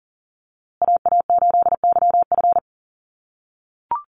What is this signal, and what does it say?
Morse code, heard as a keyed tone, about 39 wpm, 700 Hz. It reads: AA8YF
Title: USA Call Signs Repeated 3x 40wpm